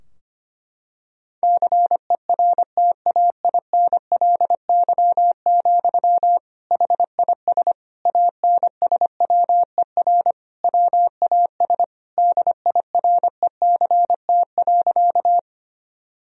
Morse code, heard as a keyed tone, about 25 wpm, 700 Hz. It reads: CERTAINLY, HIS ANSWER WAS DIRECT.
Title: Sentences from Top 1000 Words - Review Part 2 - 25wpm